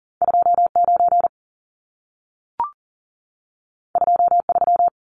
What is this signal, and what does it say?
Morse code, heard as a keyed tone, about 40 wpm, 700 Hz. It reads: 29 23
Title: Numbers - 2-Digit 40wpm